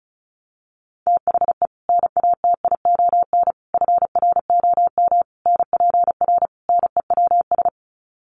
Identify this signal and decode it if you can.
Morse code, heard as a keyed tone, about 34 wpm, 700 Hz. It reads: THE NATION FROM NPR NEWS
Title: Sets of 5 Words 35wpm